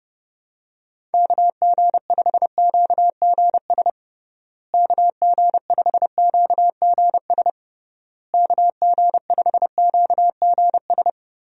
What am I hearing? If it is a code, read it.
Morse code, heard as a keyed tone, about 30 wpm, 700 Hz. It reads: KG5QGS KG5QGS KG5QGS
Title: USA Call Signs - 6 Characters - Repeated 3x - 30wpm